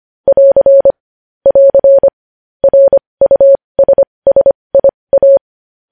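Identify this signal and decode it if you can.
Morse code, heard as a keyed tone, about 25 wpm, spaced as for 20 wpm, 550 Hz. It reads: <AR> <AR> RUSSIA
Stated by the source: News headlines sent in Morse code at 20.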